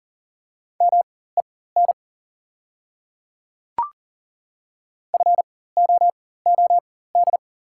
Morse code, heard as a keyed tone, about 39 wpm, spaced as for 19 wpm, 700 Hz. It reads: MEN FOOD
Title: 4x Character Spacing - Top 300 Words - 40wpm